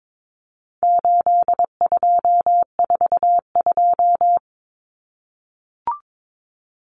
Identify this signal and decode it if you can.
Morse code, heard as a keyed tone, about 22 wpm, 700 Hz. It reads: 8242
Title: Common Cut Numbers - 4-Digit 22wpm